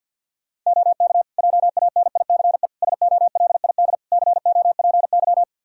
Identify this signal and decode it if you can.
Morse code, heard as a keyed tone, about 53 wpm, 700 Hz. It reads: OK JANICE SOLID COPY
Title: K1USN SST QSO Practice 50wpm